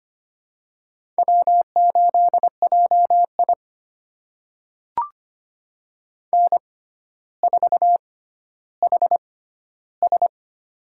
Morse code, heard as a keyed tone, about 25 wpm, spaced as for 13 wpm, 700 Hz. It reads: W8JI N4HS
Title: USA Call Signs - 4 Characters - 6x Character Spacing - 25wpm